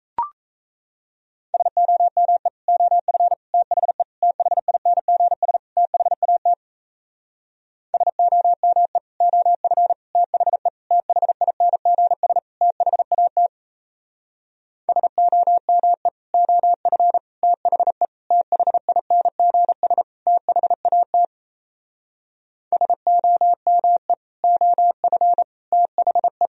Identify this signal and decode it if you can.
Morse code, heard as a keyed tone, about 37 wpm, 700 Hz. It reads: SOME OF THE THINGS THAT SOME OF THE THINGS THAT SOME OF THE THINGS THAT SOME OF THE